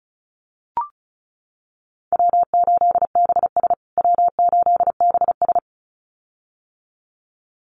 Morse code, heard as a keyed tone, about 34 wpm, 700 Hz. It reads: W8BS W8BS
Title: USA Call Signs Repeated 2x 35wpm